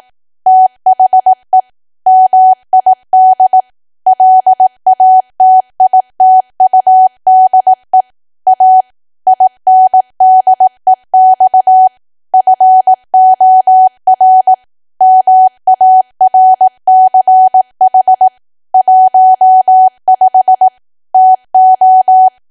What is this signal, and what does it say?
Morse code, heard as a keyed tone, about 18 wpm, 750 Hz. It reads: THE MID LATITUDE A INDEX FOR MARCH 15 TO
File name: cwa / W1AW-Level2 / 18WPM